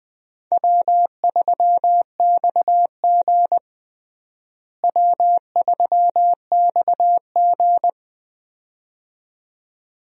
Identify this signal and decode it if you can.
Morse code, heard as a keyed tone, about 20 wpm, 700 Hz. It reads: W3XG W3XG
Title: USA Call Signs - 4 Characters - Repeated 2x - 20wpm